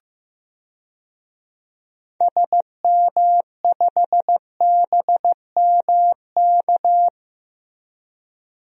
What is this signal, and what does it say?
Morse code, heard as a keyed tone, about 15 wpm, 700 Hz. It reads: SM5BMK
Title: DXCC-100 Call Signs - 6 Characters - Repeated 2x - 15wpm